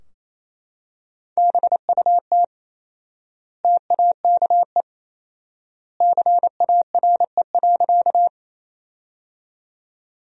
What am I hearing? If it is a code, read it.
Morse code, heard as a keyed tone, about 27 wpm, 700 Hz. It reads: BUT TAKE CARE.
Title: 4x Word-Spacing - Sentences from Top 300 Words - 28wpm